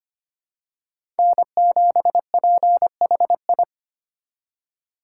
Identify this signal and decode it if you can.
Morse code, heard as a keyed tone, about 25 wpm, 700 Hz. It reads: N7PHI
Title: K1USN SST QSO Practice - 25wpm